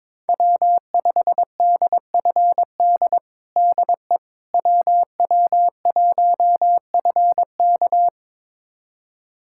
Morse code, heard as a keyed tone, about 22 wpm, 700 Hz. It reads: W5DFD DE WW1FK
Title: USA Callsign DE Callsign - 5 Characters - 22wpm